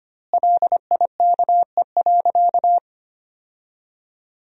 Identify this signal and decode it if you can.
Morse code, heard as a keyed tone, about 25 wpm, 700 Hz. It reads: LIKE.
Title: 2x Word-Spacing - Sentences from Top 500 Words - 25wpm